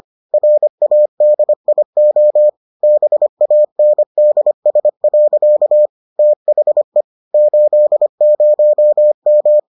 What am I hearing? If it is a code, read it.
Morse code, heard as a keyed tone, about 25 wpm, 600 Hz. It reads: RADIO BANDS. THE 80M